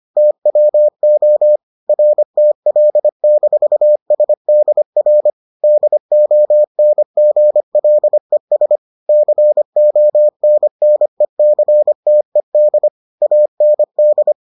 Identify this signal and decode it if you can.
Morse code, heard as a keyed tone, about 25 wpm, 600 Hz. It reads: TWO RTL-SDR DONGLES CONNECTED AND